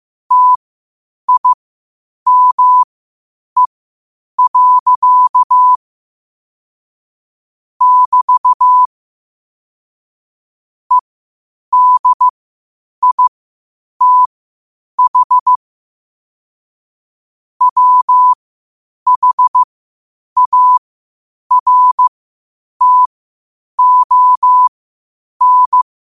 Morse code, heard as a keyed tone, about 15 wpm, spaced as for 8 wpm, 1000 Hz. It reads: TIME. <BT> EDITH WHARTON
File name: qotdfiles / 8wpm